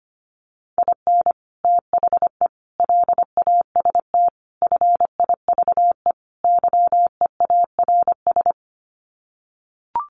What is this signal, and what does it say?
Morse code, heard as a keyed tone, about 25 wpm, 700 Hz. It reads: IN THE LAST FIVE YEARS
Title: Sets of 5 Words Encore 25wpm